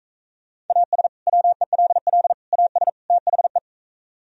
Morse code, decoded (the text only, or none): AS WELL AS THE